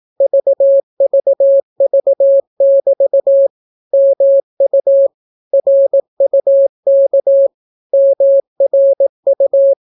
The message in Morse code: VVV<BT> MU RUK MRU

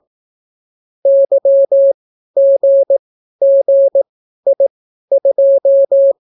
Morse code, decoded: YGGI2

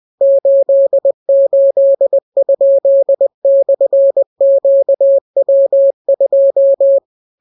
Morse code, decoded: 88?/QW2

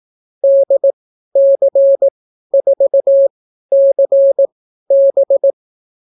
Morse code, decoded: DC4CB

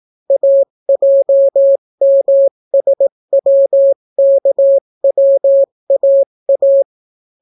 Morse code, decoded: AJMSWKWAA

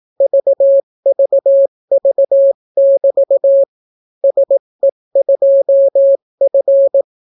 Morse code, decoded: VVV<BT> SE2F

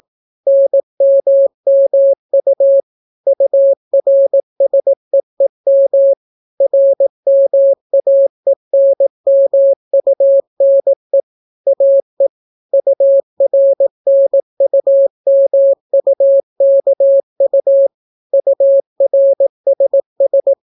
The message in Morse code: NMMU URSEEM RMAENMUNE AE URNUMUKU URSS

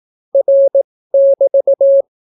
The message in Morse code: R<BT>

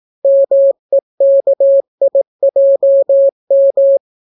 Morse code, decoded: MEKIJM

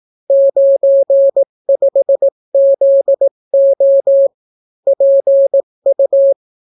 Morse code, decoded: 95ZO PU

